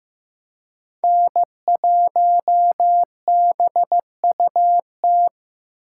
Morse code, heard as a keyed tone, about 15 wpm, 700 Hz. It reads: N1BUT
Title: USA Call Signs - 5 Characters - 8x Character Spacing - 15wpm